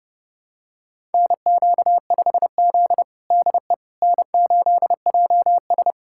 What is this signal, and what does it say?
Morse code, heard as a keyed tone, about 30 wpm, 700 Hz. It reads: NQ5Z DE N8JS
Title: USA Callsign DE Callsign - 4 Characters - 30wpm